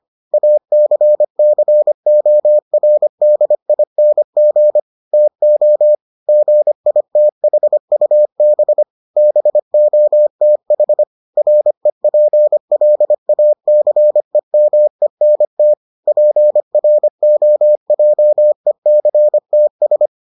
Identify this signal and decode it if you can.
Morse code, heard as a keyed tone, about 25 wpm, 600 Hz. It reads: ACCORDING TO GITHUB BOTH REPLACEMENT PROJECTS